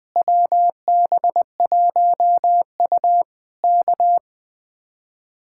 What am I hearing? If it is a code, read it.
Morse code, heard as a keyed tone, about 20 wpm, 700 Hz. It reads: WB1U K
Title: Speed-Racing Common QSO Phrases Encore - 17wpm